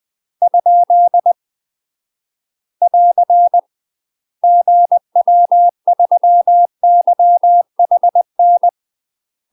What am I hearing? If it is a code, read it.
Morse code, heard as a keyed tone, about 20 wpm, 700 Hz. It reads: ? <AR> GW3YHN
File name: UK MASTER QSOs TRAINING / 20WPM / UK MASTER QSOs 0014